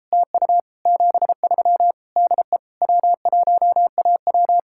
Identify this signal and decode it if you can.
Morse code, heard as a keyed tone, about 34 wpm, 700 Hz. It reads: TU 73 DE W1AW
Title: Speed-Racing Common QSO Phrases Encore - 28wpm